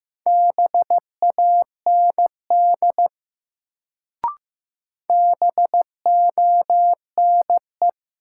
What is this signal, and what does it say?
Morse code, heard as a keyed tone, about 15 wpm, 700 Hz. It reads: BAND BONE
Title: Sentences from Top 1600 Words - New Word Drill Alt - 15wpm